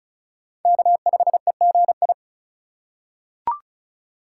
Morse code, heard as a keyed tone, about 36 wpm, 690 Hz. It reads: K5EGI
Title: USA Call Signs - 5 Characters - 3x Character Spacing - 35wpm